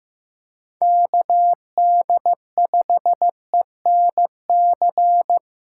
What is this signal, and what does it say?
Morse code, heard as a keyed tone, about 15 wpm, 700 Hz. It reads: KD5ENC
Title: USA Callsign DE Callsign - 6 Characters - 15wpm